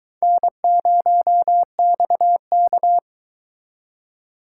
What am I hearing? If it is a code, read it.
Morse code, heard as a keyed tone, about 22 wpm, 700 Hz. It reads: N0XK